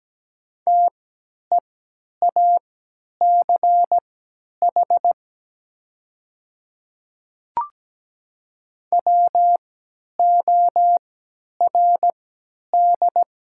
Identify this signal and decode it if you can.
Morse code, heard as a keyed tone, about 17 wpm, spaced as for 10 wpm, 700 Hz. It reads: TEACH WORD
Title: 3x Character Spacing - Top 500 Words - 17wpm